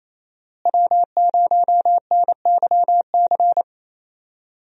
Morse code, heard as a keyed tone, about 28 wpm, 700 Hz. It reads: W0NYC
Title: USA Call Signs - 5 Characters - Repeated 2x - 28wpm